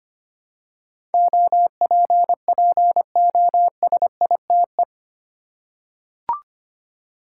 Morse code, heard as a keyed tone, about 25 wpm, 700 Hz. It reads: OPPOSITE
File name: 8 Letter Words - First 3 Letters Given - 25wpm